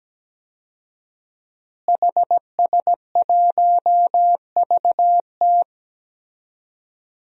Morse code, heard as a keyed tone, about 17 wpm, 700 Hz. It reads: HS1VT
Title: DXCC-100 Call Signs - 5 Characters - Repeated 3x - 17wpm